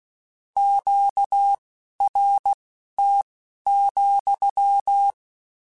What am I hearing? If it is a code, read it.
Morse code, heard as a keyed tone, about 16 wpm, spaced as for 12 wpm, 800 Hz. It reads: QRT,
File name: Q Calls / Q Calls-mp3s / Q Calls sppc @15